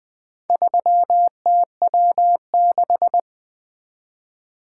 Morse code, heard as a keyed tone, about 20 wpm, 700 Hz. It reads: 3TW6
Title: Four Letter-Number Combinations - 20wpm